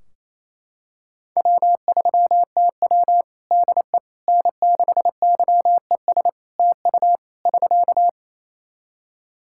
Morse code, heard as a keyed tone, about 27 wpm, 700 Hz. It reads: W3TW DE N6YES TU <SK>
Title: Practice QSO - Exchange 4 - Ending - 28wpm